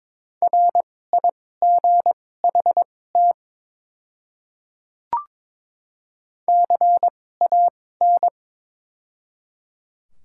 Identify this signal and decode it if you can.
Morse code, heard as a keyed tone, about 22 wpm, spaced as for 16 wpm, 700 Hz. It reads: RIGHT CAN
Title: 2x Character Spacing - Top 200 Words - 22wpm